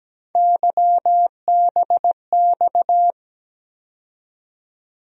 Morse code, heard as a keyed tone, about 17 wpm, 700 Hz. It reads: YBX
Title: USA Call Sign Postfixes - 3 Letters 17wpm